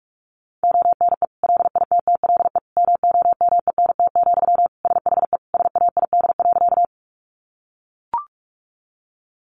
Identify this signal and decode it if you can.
Morse code, heard as a keyed tone, about 44 wpm, 700 Hz. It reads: ONE LITTLE MOMENT, SHE SAID.
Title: Sentences from Top 1000 Words 45wpm